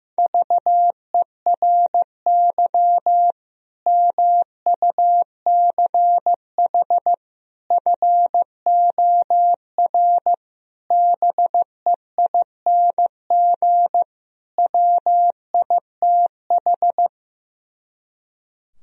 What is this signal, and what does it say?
Morse code, heard as a keyed tone, about 15 wpm, 700 Hz. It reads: VERY MUCH FOR BEING WITH
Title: Speed-Racing - Sets of 5 Words 15wpm